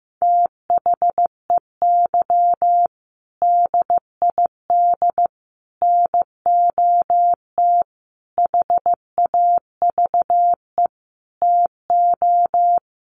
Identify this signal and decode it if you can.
Morse code, heard as a keyed tone, about 15 wpm, 700 Hz. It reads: THEY DID NOT HAVE TO
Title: Sets of 5 Words Encore2 15wpm